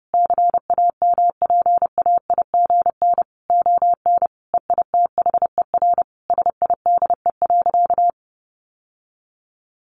Morse code, heard as a keyed tone, about 30 wpm, 700 Hz. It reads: CAMPAIGN ON EITHER SIDE.
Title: Sentences from Top 1200 Words 30wpm